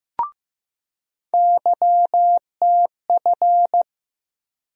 Morse code, heard as a keyed tone, about 15 wpm, 700 Hz. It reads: YTF
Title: USA Call Sign Postfixes - 3 Letters 15wpm